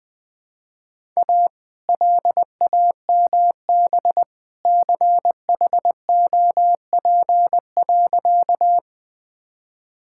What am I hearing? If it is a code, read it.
Morse code, heard as a keyed tone, about 20 wpm, 700 Hz. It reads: A LAMB CHOP.
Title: All-Age Appropriate Jokes - 20wpm